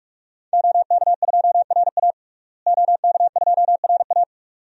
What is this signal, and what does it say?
Morse code, heard as a keyed tone, about 46 wpm, 700 Hz. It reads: OK2RA OK2RA
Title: DXCC-100 Call Signs - 5 Characters - Repeated 2x - 45wpm